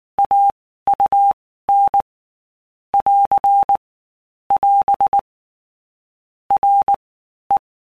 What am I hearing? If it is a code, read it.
Morse code, heard as a keyed tone, about 19 wpm, 800 Hz. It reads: A U N <AR> <AS> R E
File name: Basic Letter Groups By Blocks / Groups@18 / AWUNDG@18